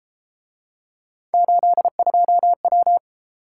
Morse code, heard as a keyed tone, about 30 wpm, 700 Hz. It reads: 82W